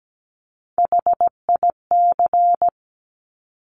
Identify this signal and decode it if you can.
Morse code, heard as a keyed tone, about 17 wpm, 700 Hz. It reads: HIC
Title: Top Three Letter Combinations in Words 17wpm